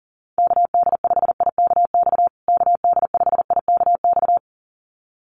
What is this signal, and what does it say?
Morse code, heard as a keyed tone, about 40 wpm, 700 Hz. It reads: KD5IKX KD5IKX
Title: USA Call Signs Repeated 2x 40wpm